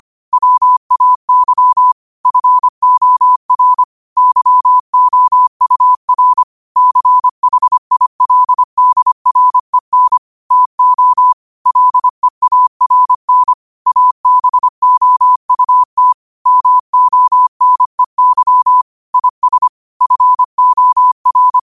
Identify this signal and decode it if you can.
Morse code, heard as a keyed tone, about 25 wpm, 1000 Hz. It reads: WAY FOR YOUR CHILDREN TO LEARN ABOUT MONEY IS FOR
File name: qotdfiles / 25wpm